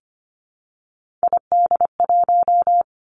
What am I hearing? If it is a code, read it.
Morse code, heard as a keyed tone, about 25 wpm, 700 Hz. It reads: ID1
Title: Three Challenging Letter-Number Combinations 25wpm